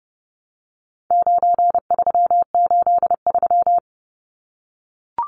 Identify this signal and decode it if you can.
Morse code, heard as a keyed tone, about 30 wpm, 700 Hz. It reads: 9383
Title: Numbers - 4-Digit 30wpm